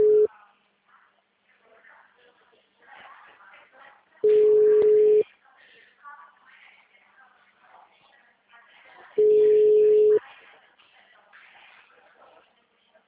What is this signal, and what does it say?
Ringback tone (Europe, 425 Hz), with speech.
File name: out- -2003- - - .524